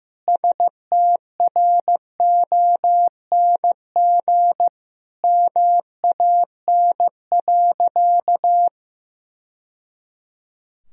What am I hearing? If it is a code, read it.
Morse code, heard as a keyed tone, about 15 wpm, 700 Hz. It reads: STRONG MAN.
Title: Sentences from Top 1000 Words - Review Part 3 - 15wpm